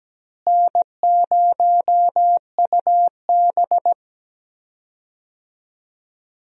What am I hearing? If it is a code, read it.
Morse code, heard as a keyed tone, about 17 wpm, 700 Hz. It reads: N0UB
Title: USA Call Signs - 4 Characters - Repeated 2x - 17wpm